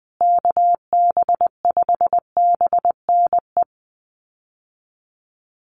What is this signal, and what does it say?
Morse code, heard as a keyed tone, about 20 wpm, 700 Hz. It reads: KB5BNE
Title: USA Call Signs Repeated 3x 20wpm